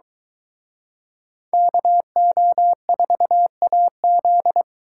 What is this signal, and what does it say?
Morse code, heard as a keyed tone, about 23 wpm, 700 Hz. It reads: KO4AZ